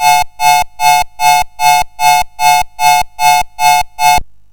There is no code.